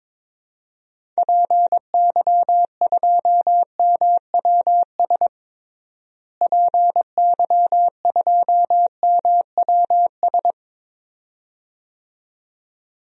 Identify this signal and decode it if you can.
Morse code, heard as a keyed tone, about 22 wpm, 700 Hz. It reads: PY2MWS PY2MWS
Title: DXCC-100 Call Signs - 6 Characters - Repeated 2x - 22wpm